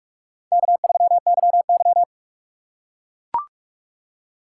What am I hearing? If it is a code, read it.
Morse code, heard as a keyed tone, about 43 wpm, 700 Hz. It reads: K3YY
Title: USA Call Signs - 6x Character Spacing - 45wpm